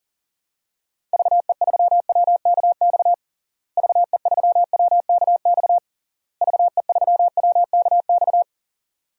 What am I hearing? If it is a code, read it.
Morse code, heard as a keyed tone, about 40 wpm, 700 Hz. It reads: VE3WKX VE3WKX VE3WKX
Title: DXCC-100 Call Signs - 6 Characters - Repeated 3x - 40wpm